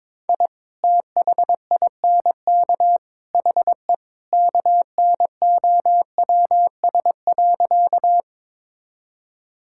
Morse code, heard as a keyed tone, about 22 wpm, 700 Hz. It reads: I THINK HE KNOWS.
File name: practice / 22wpm / Sentences with 3-Word Phrases - 22wpm